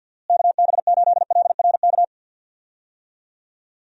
Morse code, heard as a keyed tone, about 51 wpm, 700 Hz. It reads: KB8LRK
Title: USA Call Signs - 6 Characters - Repeated 3x - 50wpm